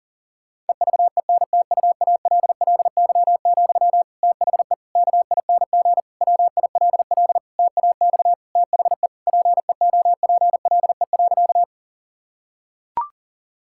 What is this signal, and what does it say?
Morse code, heard as a keyed tone, about 37 wpm, 700 Hz. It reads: EVENTUALLY, THE KING WILL TAX THE PEOPLE.
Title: Sentences from Top 1000 Words - Review Part 6 - 40wpm